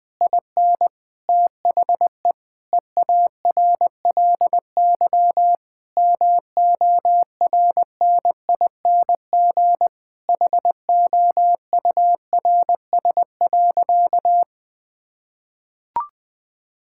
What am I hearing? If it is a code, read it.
Morse code, heard as a keyed tone, about 20 wpm, 700 Hz. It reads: IN THE EARLY MORNING HOURS.
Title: Difficult Sentences with Hints - 20wpm